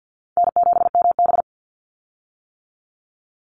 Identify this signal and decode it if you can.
Morse code, heard as a keyed tone, about 50 wpm, 700 Hz. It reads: N7MB
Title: Call Signs - 50wpm